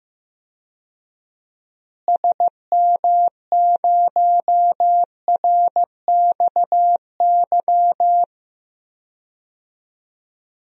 Morse code, heard as a keyed tone, about 15 wpm, 700 Hz. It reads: SM0RXY
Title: DXCC-100 Call Signs - 6 Characters - Repeated 3x - 15wpm